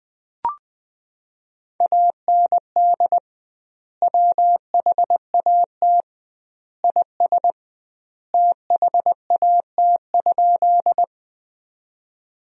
Morse code, heard as a keyed tone, about 20 wpm, 700 Hz. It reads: AND WHAT IS THAT?
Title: 2x Word-Spacing - Sentences from Top 100 Words - 20wpm